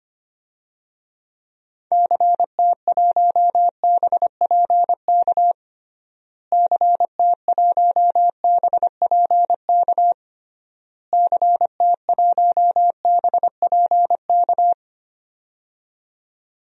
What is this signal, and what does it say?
Morse code, heard as a keyed tone, about 25 wpm, 700 Hz. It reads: CT1BPK CT1BPK CT1BPK
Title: DXCC-100 Call Signs - 6 Characters - Repeated 3x - 25wpm